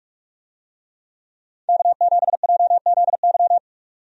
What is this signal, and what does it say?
Morse code, heard as a keyed tone, about 43 wpm, 700 Hz. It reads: K7JZY
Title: USA Call Signs - 5 Characters - 45wpm